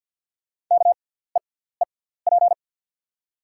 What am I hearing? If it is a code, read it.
Morse code, heard as a keyed tone, about 50 wpm, spaced as for 17 wpm, 690 Hz. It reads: KEEP
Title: 6x Character Spacing - Top 500 Words - 50wpm